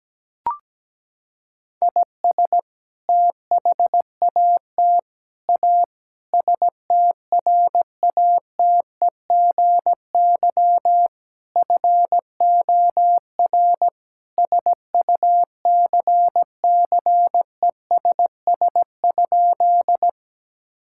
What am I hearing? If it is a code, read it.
Morse code, heard as a keyed tone, about 17 wpm, 700 Hz. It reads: IS THAT A STRATEGY FOR SUCCESS?
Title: Sentences from Top 1000 Words - Review Part 5 - 17wpm